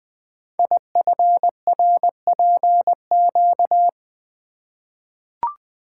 Morse code, heard as a keyed tone, about 20 wpm, 700 Hz. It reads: IFRPQ
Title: Code Groups - 5 Letter:Numbers - 20wpm